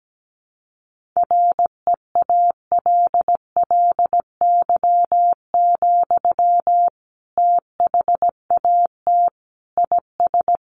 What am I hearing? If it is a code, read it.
Morse code, heard as a keyed tone, about 17 wpm, 700 Hz. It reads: REALLY, THAT IS
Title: Sentences from Top 600 Words 17wpm